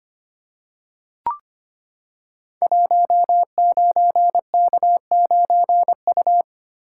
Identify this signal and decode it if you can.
Morse code, heard as a keyed tone, about 25 wpm, 700 Hz. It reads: 19K9U
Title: Code Groups - 5 Letter:Numbers - 25wpm